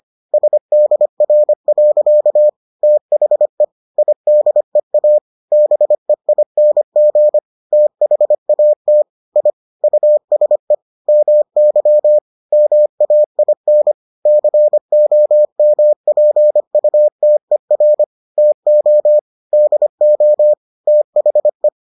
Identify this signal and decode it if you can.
Morse code, heard as a keyed tone, about 25 wpm, 600 Hz. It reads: SDR. THE IDEA BEING THAT I USE MY MAIN COMPUTER TO DO THE